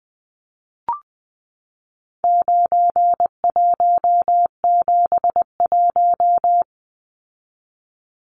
Morse code, heard as a keyed tone, about 20 wpm, 700 Hz. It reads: 9171
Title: Numbers - 4-Digit 20wpm